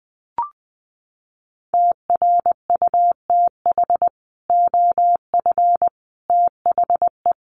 Morse code, heard as a keyed tone, about 20 wpm, 700 Hz. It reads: TRUTH OF THE
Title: Sets of 5 Words Encore2 20wpm